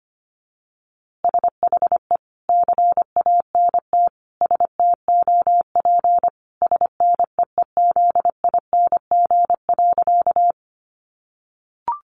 Morse code, heard as a keyed tone, about 25 wpm, 700 Hz. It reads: SHE CANT STOP SNEEZING.
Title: Easy Phrases 25wpm